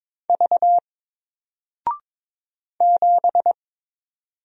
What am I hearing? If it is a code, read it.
Morse code, heard as a keyed tone, about 22 wpm, 700 Hz. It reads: V 7
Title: Single Letter-Number Alt 22wpm